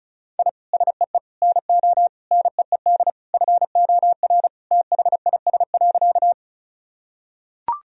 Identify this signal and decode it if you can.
Morse code, heard as a keyed tone, about 34 wpm, 690 Hz. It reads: I SEE NO NEED FOR THIS.
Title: Sentences from Top 1000 Words - Review Part 1 - 35wpm